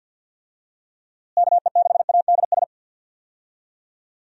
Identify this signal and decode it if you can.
Morse code, heard as a keyed tone, about 51 wpm, 700 Hz. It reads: KE6ADS
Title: USA Call Signs - 6 Characters - Repeated 2x - 50wpm